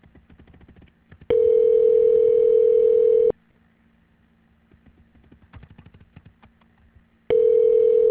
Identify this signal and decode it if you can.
Ringback tone (North America), 440 + 480 Hz.